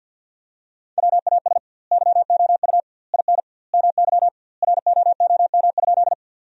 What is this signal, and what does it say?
Morse code, heard as a keyed tone, about 52 wpm, 700 Hz. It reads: WAS YOU IN MY ROOM?
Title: 2x Word-Spacing - Sentences from Top 300 Words - 50wpm